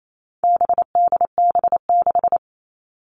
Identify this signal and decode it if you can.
Morse code, heard as a keyed tone, about 28 wpm, 700 Hz. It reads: BDB6
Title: All Cut Numbers - 4-Digit 28wpm